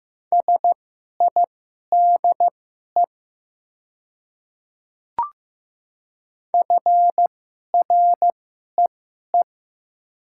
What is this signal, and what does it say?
Morse code, heard as a keyed tone, about 15 wpm, spaced as for 11 wpm, 700 Hz. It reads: SIDE FREE
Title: 2x Character Spacing - Top 500 Words - 15wpm